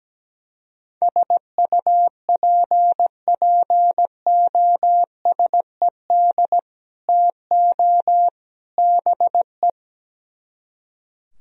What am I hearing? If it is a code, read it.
Morse code, heard as a keyed tone, about 17 wpm, 700 Hz. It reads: SUPPOSED TO BE
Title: Speed-Racing - Sets of 3 Words 17wpm